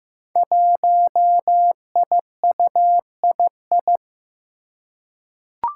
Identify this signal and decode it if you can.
Morse code, heard as a keyed tone, about 15 wpm, 700 Hz. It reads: 1IUII
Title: Five Challenging Letter-Number Combinations 15wpm